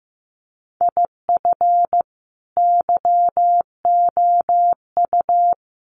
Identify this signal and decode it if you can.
Morse code, heard as a keyed tone, about 15 wpm, 700 Hz. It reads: IF YOU
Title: Sets of 5 Words 15wpm